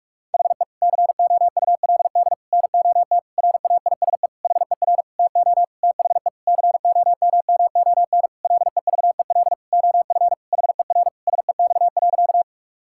Morse code, heard as a keyed tone, about 42 wpm, 700 Hz. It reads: HE COULD NOT RAISE HER TO THE COMMON LEVEL OF HER SEX.
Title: Sentences from Top 1600 Words 45wpm